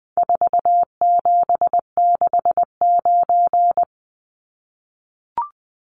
Morse code, heard as a keyed tone, about 20 wpm, 700 Hz. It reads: 4769
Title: Numbers - 4-Digit 20wpm